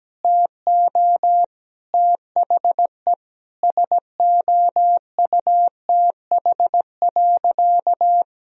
Morse code, heard as a keyed tone, about 17 wpm, 700 Hz. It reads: TO THE SOUTH.
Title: Sentences from Top 1000 Words - Review Part 2 - 17wpm